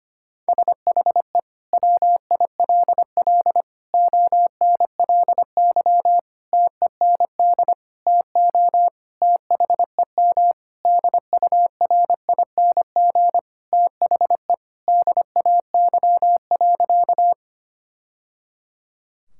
Morse code, heard as a keyed tone, about 25 wpm, 700 Hz. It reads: SHE WILL ONLY TEND TO THEM DURING THE DAY.
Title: Sentences from Top 1000 Words - Review Part 5 - 25wpm